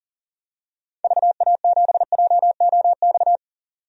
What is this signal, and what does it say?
Morse code, heard as a keyed tone, about 37 wpm, 700 Hz. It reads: VA7JOX
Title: DXCC-100 Call Signs - 6 Characters - 40wpm